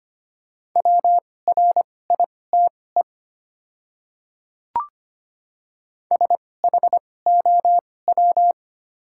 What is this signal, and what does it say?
Morse code, heard as a keyed tone, about 25 wpm, spaced as for 18 wpm, 700 Hz. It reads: WRITE SHOW
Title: 2x Character Spacing - Top 200 Words - 25wpm